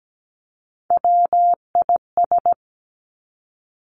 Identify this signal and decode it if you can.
Morse code, heard as a keyed tone, about 17 wpm, 700 Hz. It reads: WIS
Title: Three Challenging Letter-Number Combinations 17wpm